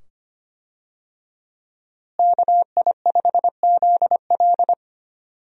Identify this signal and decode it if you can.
Morse code, heard as a keyed tone, about 25 wpm, 700 Hz. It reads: KI5ZL
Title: USA Call Signs - 5 Characters - Repeated 2x - 25wpm